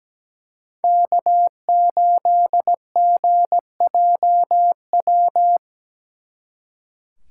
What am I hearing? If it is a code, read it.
Morse code, heard as a keyed tone, about 20 wpm, 700 Hz. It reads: K8GJW
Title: Speed Racing - USA 1x3 Call Signs 17wpm